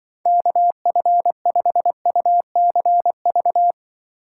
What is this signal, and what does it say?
Morse code, heard as a keyed tone, about 25 wpm, 700 Hz. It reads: KF5UCV